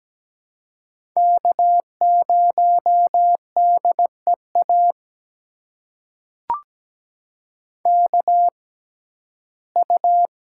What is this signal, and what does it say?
Morse code, heard as a keyed tone, about 17 wpm, spaced as for 12 wpm, 700 Hz. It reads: K0DEA KU
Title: USA Call Signs - 5 Characters - 6x Character Spacing - 17wpm